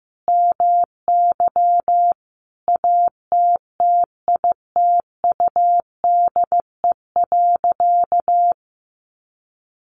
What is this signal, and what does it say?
Morse code, heard as a keyed tone, about 15 wpm, 700 Hz. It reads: MY ATTITUDE.
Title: Sentences from Top 1300 Words 15wpm